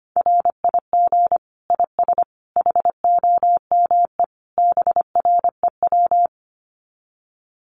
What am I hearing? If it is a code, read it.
Morse code, heard as a keyed tone, about 25 wpm, 700 Hz. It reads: RIG IS HOME BREW
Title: Common QSO Phrases 25wpm